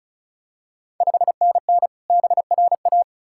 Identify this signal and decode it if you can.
Morse code, heard as a keyed tone, about 36 wpm, 700 Hz. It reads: 5NN BRA
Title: Ontario QSO Party Exchange 35wpm